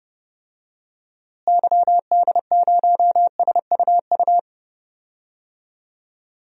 Morse code, heard as a keyed tone, about 30 wpm, 700 Hz. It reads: YD0SUU
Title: DXCC-100 Call Signs - 6 Characters - Repeated 2x - 30wpm